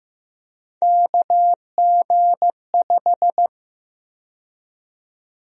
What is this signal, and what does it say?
Morse code, heard as a keyed tone, about 15 wpm, 700 Hz. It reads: KG5
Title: USA Call Sign Prefixes - 15wpm